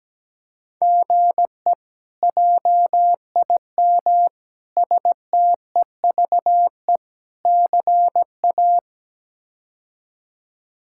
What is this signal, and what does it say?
Morse code, heard as a keyed tone, about 17 wpm, 700 Hz. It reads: GE JIM STEVE CA
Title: K1USN SST QSO Practice - 17wpm